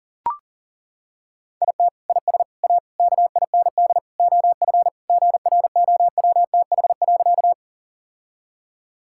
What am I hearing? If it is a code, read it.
Morse code, heard as a keyed tone, about 40 wpm, 700 Hz. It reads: IT IS A KIND OF GROWTH.
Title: Sentences from Top 1000 Words - Review Part 5 - 40wpm